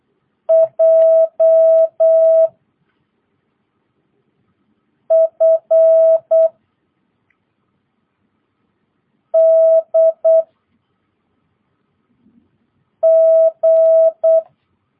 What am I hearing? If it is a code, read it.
Morse code, heard as a keyed tone, about 8 wpm, 650 Hz. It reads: J F D G
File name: jh8cbh / 2022cw